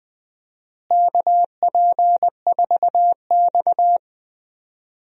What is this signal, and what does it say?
Morse code, heard as a keyed tone, about 20 wpm, 700 Hz. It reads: KP4X
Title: DXCC-100 Call Signs - 4 Characters - Repeated 3x - 20wpm